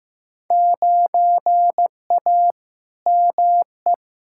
Morse code, heard as a keyed tone, about 15 wpm, 700 Hz. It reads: 9A ME
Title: Field Day Exchanges Encore - 15wpm